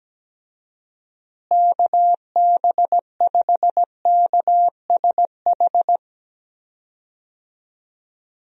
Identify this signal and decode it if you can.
Morse code, heard as a keyed tone, about 17 wpm, 700 Hz. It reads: KB5KSH
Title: USA Call Signs - 6 Characters - Repeated 2x - 17wpm